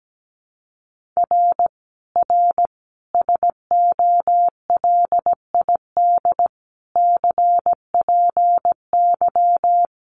Morse code, heard as a keyed tone, about 17 wpm, 700 Hz. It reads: R R SOLID CPY
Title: Common QSO Phrases 17wpm